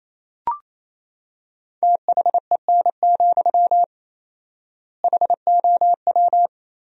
Morse code, heard as a keyed tone, about 27 wpm, 700 Hz. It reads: THEN, HOW
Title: 4x Word-Spacing - Sentences from Top 100 Words - 28wpm